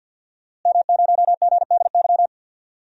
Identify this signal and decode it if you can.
Morse code, heard as a keyed tone, about 52 wpm, 700 Hz. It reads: M0GDY
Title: DXCC-100 Call Signs - 5 Characters - Repeated 2x - 50wpm